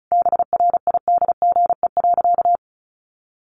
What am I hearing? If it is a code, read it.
Morse code, heard as a keyed tone, about 34 wpm, 700 Hz. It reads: BRIDGE.
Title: Sentences from Top 1200 Words 35wpm